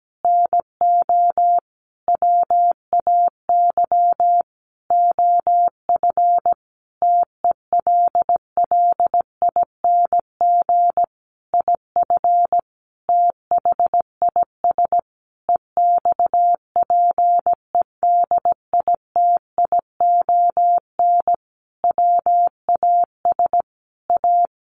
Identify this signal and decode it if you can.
Morse code, heard as a keyed tone, about 17 wpm, 700 Hz. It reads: NO WAY OF TELLING IF THIS EXPEDITION WAS A
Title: A Sound Of Thunder 17wpm